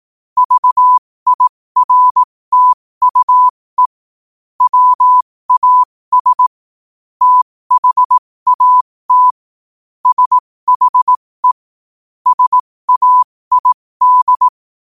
Morse code, heard as a keyed tone, about 18 wpm, spaced as for 15 wpm, 1000 Hz. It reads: VIRTUE WAS THAT SHE SAID